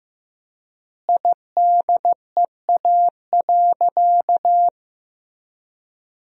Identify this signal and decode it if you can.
Morse code, heard as a keyed tone, about 15 wpm, 700 Hz. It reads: IDEA.
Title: 4x Word-Spacing - Sentences from Top 300 Words - 15wpm